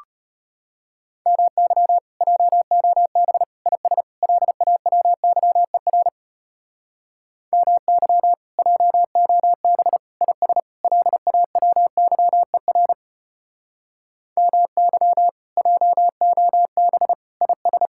Morse code, heard as a keyed tone, about 32 wpm, 700 Hz. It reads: MY JOB IS LAWYER MY JOB IS LAWYER MY JOB IS